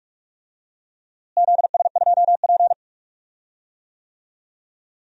Morse code, heard as a keyed tone, about 46 wpm, 700 Hz. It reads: ZS2P
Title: DXCC-100 Call Signs - 4 Characters - Repeated 2x - 45wpm